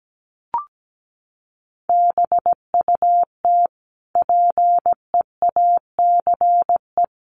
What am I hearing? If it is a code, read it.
Morse code, heard as a keyed tone, about 17 wpm, 700 Hz. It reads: BUT PEACE
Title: Sentences from Top 1500 Words 17wpm